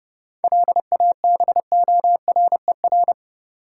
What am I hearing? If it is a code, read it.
Morse code, heard as a keyed tone, about 30 wpm, 700 Hz. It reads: LABORER
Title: Practice QSO - Exchange 3 - More Info - 30wpm